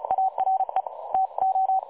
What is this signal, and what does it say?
Morse code, heard as a keyed tone, about 33 wpm, 760 Hz. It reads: ARE TO
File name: dlazaawansowanych / nagraniazpasma / websdr recording 3565.0kHz